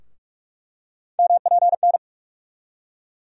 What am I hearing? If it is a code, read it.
Morse code, heard as a keyed tone, about 41 wpm, 700 Hz. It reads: MPN
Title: USA Call Sign Postfixes - 3 Letters 45wpm